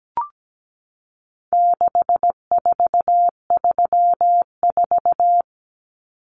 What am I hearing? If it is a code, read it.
Morse code, heard as a keyed tone, about 17 wpm, 700 Hz. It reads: 6434
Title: Numbers - 4-Digit 17wpm